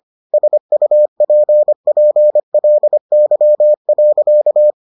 Morse code, heard as a keyed tone, about 25 wpm, 600 Hz. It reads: SUPPLY.